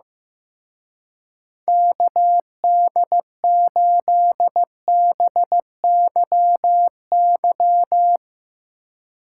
Morse code, heard as a keyed tone, about 16 wpm, 700 Hz. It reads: KD8BYY
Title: Speed Racing - USA 2x3 Call Signs 15wpm